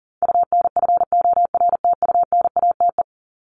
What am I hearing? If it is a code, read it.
Morse code, heard as a keyed tone, about 38 wpm, 700 Hz. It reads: UNFORTUNATE
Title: LY Suffix - Top 100 Words with Base Words 40wpm